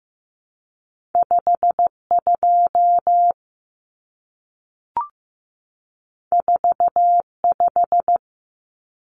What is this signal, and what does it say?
Morse code, heard as a keyed tone, about 15 wpm, 700 Hz. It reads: 52 45
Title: Numbers - 2-Digit 15wpm